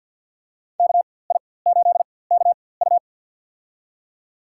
Morse code, heard as a keyed tone, about 46 wpm, spaced as for 23 wpm, 700 Hz. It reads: KI8KU
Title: USA Call Signs - 5 Characters - 4x Character Spacing - 50wpm